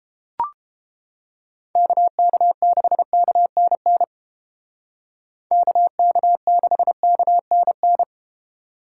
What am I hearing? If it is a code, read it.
Morse code, heard as a keyed tone, about 32 wpm, 700 Hz. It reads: KK6KNN KK6KNN